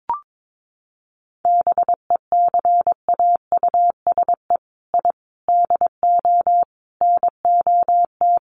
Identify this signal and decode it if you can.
Morse code, heard as a keyed tone, about 22 wpm, 700 Hz. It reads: BECAUSE I DO NOT
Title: Sets of 5 Words Encore 22wpm